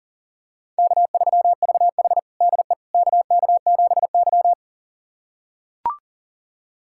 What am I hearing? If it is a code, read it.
Morse code, heard as a keyed tone, about 40 wpm, 700 Hz. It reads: K3VH DE KK7Y
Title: USA Callsign DE Callsign - 4 Characters - 40wpm